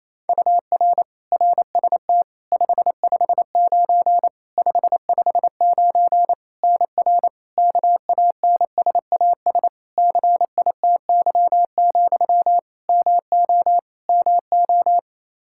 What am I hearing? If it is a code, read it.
Morse code, heard as a keyed tone, about 28 wpm, 700 Hz. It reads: UR RST 559 559 NR KANSAS CITY, MO MO
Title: K1USN SST QSO Practice 28wpm